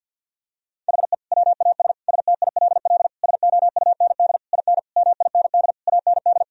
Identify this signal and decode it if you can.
Morse code, heard as a keyed tone, about 50 wpm, 700 Hz. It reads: HE WAS STILL SOUND IN MIND AND
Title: Sentences from Top 1000 Words - Review Part 2 - 50wpm